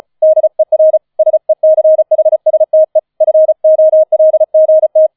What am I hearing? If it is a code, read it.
Morse code, heard as a keyed tone, about 34 wpm, 630 Hz. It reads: DER SECHSTE FOLGT
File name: 005 Fünfter Streich 35wpm